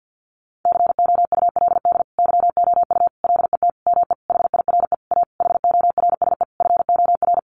Morse code, heard as a keyed tone, about 50 wpm, 700 Hz. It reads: COULD YOU LET ME HIRE A HORSE FOR
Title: Sentences from Top 1400 Words 50wpm